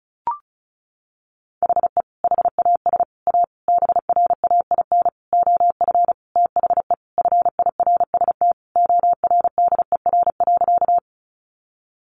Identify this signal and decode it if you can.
Morse code, heard as a keyed tone, about 34 wpm, 700 Hz. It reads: HE HAS A BRAIN OF THE FIRST ORDER.
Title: Sentences from Top 1100 Words 35wpm